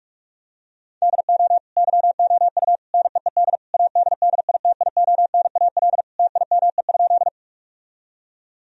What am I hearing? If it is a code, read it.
Morse code, heard as a keyed tone, about 41 wpm, 700 Hz. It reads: DO YOU NEED ADDITIONAL TIME?
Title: Sentences from Top 1000 Words - Review Part 4 - 45wpm